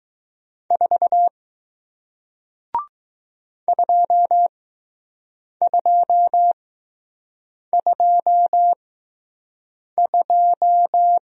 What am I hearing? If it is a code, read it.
Morse code, heard as a keyed tone, about 20 wpm, spaced as for 6 wpm, 700 Hz. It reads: 4 2222